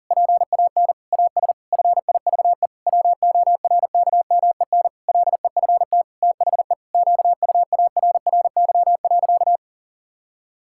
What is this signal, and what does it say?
Morse code, heard as a keyed tone, about 40 wpm, 700 Hz. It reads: PAN AS FIVE WORKMEN LEFT THE QUARRY.
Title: Pangram Challenge v2 - 40wpm